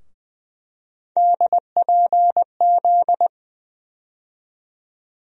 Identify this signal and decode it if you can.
Morse code, heard as a keyed tone, about 20 wpm, 700 Hz. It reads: DPZ
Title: USA Call Sign Postfixes - 3 Letters 20wpm